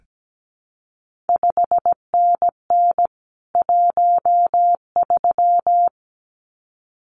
Morse code, heard as a keyed tone, about 17 wpm, 700 Hz. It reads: 5NN 13
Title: CQ WW DX Contest Exchange - 17wpm